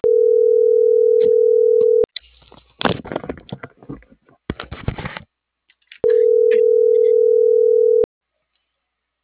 Ringback tone (North America), 440 + 480 Hz.
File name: CENTRAL MAERSK / out yunuel.morales